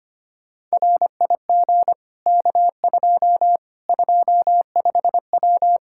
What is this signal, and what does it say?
Morse code, heard as a keyed tone, about 25 wpm, 700 Hz. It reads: RIG K2 25W
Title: Practice QSO - Exchange 2 - Info - 25wpm